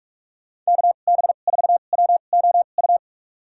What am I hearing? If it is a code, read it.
Morse code, heard as a keyed tone, about 44 wpm, spaced as for 32 wpm, 700 Hz. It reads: KB4WOU
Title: USA Call Signs - 6 Characters - 2x Character Spacing - 45wpm